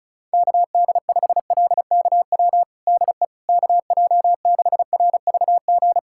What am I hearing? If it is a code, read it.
Morse code, heard as a keyed tone, about 34 wpm, 700 Hz. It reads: KD5LKW DE KJ6RVG
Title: USA Callsign DE Callsign - 6 Characters - 35wpm